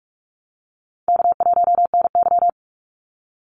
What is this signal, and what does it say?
Morse code, heard as a keyed tone, about 44 wpm, 700 Hz. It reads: K1NY
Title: Easy Call Signs Repeated 2x - 45wpm